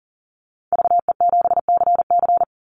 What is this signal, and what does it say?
Morse code, heard as a keyed tone, about 40 wpm, 700 Hz. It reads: VE7CC
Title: Call Signs - 40wpm